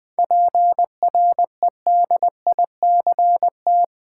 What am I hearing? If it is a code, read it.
Morse code, heard as a keyed tone, about 20 wpm, 700 Hz. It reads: PREDICT
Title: 7 Letter Words - First 3 Letters Given - 20wpm